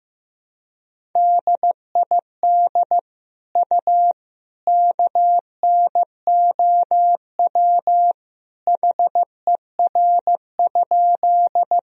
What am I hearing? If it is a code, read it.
Morse code, heard as a keyed tone, about 15 wpm, 700 Hz. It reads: DID U KNOW HER?
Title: Sentences from Top 100 Words and Abbreviations 15wpm